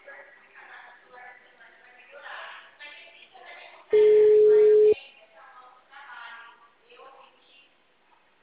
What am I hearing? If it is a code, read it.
Ringback tone (Europe, 425 Hz), with speech.